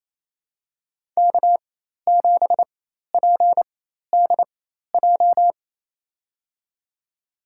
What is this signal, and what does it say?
Morse code, heard as a keyed tone, about 28 wpm, spaced as for 13 wpm, 700 Hz. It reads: K7PDJ
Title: USA Call Signs - 5 Characters - 4x Character Spacing - 28wpm